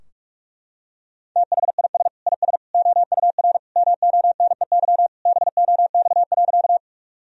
Morse code, heard as a keyed tone, about 41 wpm, 700 Hz. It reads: THIS IS OUR MONEY BOX.
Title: Sentences from Top 1000 Words - Review Part 2 - 45wpm